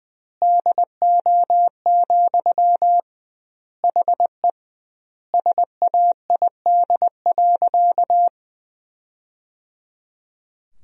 Morse code, heard as a keyed tone, about 21 wpm, 700 Hz. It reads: DO, HE SAID.
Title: 2x Word-Spacing - Sentences from Top 100 Words - 20wpm